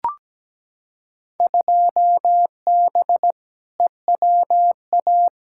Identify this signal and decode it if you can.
Morse code, heard as a keyed tone, about 17 wpm, 700 Hz. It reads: 2B EWA
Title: Field Day Exchanges Encore - 17wpm